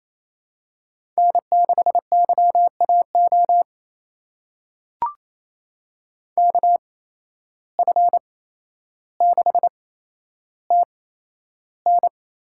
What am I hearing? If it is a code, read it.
Morse code, heard as a keyed tone, about 28 wpm, spaced as for 12 wpm, 700 Hz. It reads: N6YAO KF6TN
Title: USA Call Signs - 8x Character Spacing - 28wpm